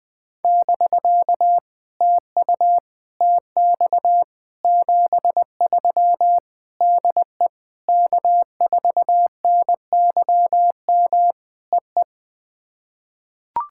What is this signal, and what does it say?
Morse code, heard as a keyed tone, about 20 wpm, 700 Hz. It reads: <BK> TU TX 73 DE K4NYM EE
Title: POTA QSO Practice - 20wpm